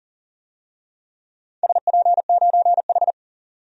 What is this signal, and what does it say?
Morse code, heard as a keyed tone, about 37 wpm, 700 Hz. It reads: SP9H
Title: DXCC-100 Call Signs - 4 Characters - 40wpm